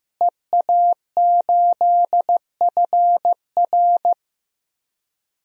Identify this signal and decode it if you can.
Morse code, heard as a keyed tone, about 15 wpm, 700 Hz. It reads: EA8FR
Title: DXCC-100 Call Signs - 5 Characters - Repeated 3x - 15wpm